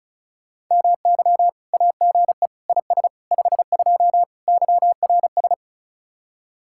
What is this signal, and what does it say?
Morse code, heard as a keyed tone, about 43 wpm, 700 Hz. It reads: MY AGE IS 52 YRS
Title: Speed-Racing Common QSO Phrases Encore - 35wpm